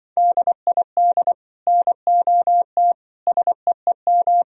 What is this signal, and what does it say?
Morse code, heard as a keyed tone, about 24 wpm, 700 Hz. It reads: DID NOT SEEM
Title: Speed-Racing - Sets of 4 Words 20wpm